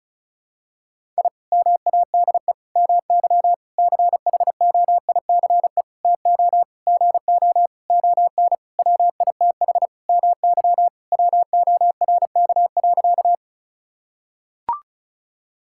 Morse code, heard as a keyed tone, about 36 wpm, 700 Hz. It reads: I MADE MY CHOICE TO GO ON WITH MY WORK.
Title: Sentences from Top 1000 Words - Review Part 6 - 35wpm